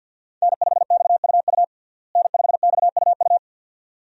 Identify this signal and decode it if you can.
Morse code, heard as a keyed tone, about 50 wpm, 700 Hz. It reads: N5XUU N5XUU
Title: USA Call Signs - 5 Characters - Repeated 2x - 50wpm